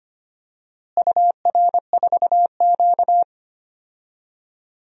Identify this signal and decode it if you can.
Morse code, heard as a keyed tone, about 25 wpm, 700 Hz. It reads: UR4Q
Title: DXCC-100 Call Signs - 4 Characters - Repeated 2x - 25wpm